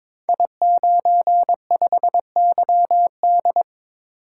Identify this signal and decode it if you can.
Morse code, heard as a keyed tone, about 22 wpm, 700 Hz. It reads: I95YD
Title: Code Groups - 5 Letter:Numbers - 22wpm